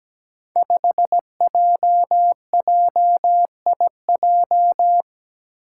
Morse code, heard as a keyed tone, about 17 wpm, 700 Hz. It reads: 5JJIJ
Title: Five Challenging Letter-Number Combinations 17wpm